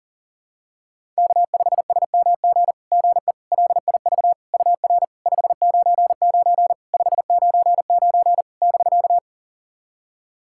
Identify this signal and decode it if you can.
Morse code, heard as a keyed tone, about 42 wpm, 700 Hz. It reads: K5SMG GE LIV UR 599 599 <BK>
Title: SOTA QSO Practice 40wpm